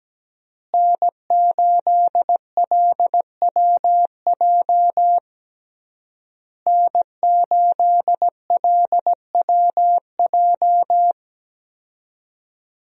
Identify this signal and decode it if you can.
Morse code, heard as a keyed tone, about 17 wpm, 700 Hz. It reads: N8LWJ N8LWJ
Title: USA Call Signs - 5 Characters - Repeated 3x - 17wpm